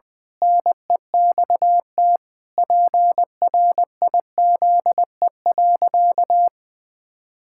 Morse code, heard as a keyed tone, about 20 wpm, 700 Hz. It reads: NEXT PRIZE.
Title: Pangram Challenge v2 - 20wpm